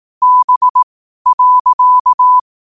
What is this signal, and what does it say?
Morse code, heard as a keyed tone, about 18 wpm, spaced as for 12 wpm, 1000 Hz. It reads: B.
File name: qotdfiles / 12wpm